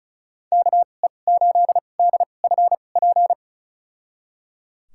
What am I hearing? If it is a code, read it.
Morse code, heard as a keyed tone, about 36 wpm, spaced as for 25 wpm, 700 Hz. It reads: KE8DFP
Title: USA Call Signs - 6 Characters - 2x Character Spacing - 35wpm